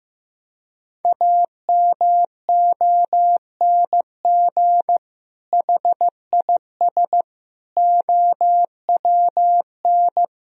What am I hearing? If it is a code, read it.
Morse code, heard as a keyed tone, about 16 wpm, 700 Hz. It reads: AMONG HIS OWN
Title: Sentences from Top 1000 Words - Review Part 3 - 15wpm